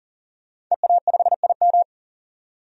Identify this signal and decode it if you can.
Morse code, heard as a keyed tone, about 40 wpm, 700 Hz. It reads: EA5IM
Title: DXCC-100 Call Signs - 5 Characters - 40wpm